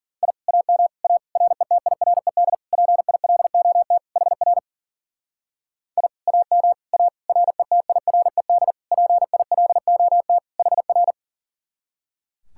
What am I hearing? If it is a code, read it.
Morse code, heard as a keyed tone, about 49 wpm, 700 Hz. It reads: I AM A RETIRED PILOT HR I AM A RETIRED PILOT HR
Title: Speed-Racing Common QSO Phrases - 40wpm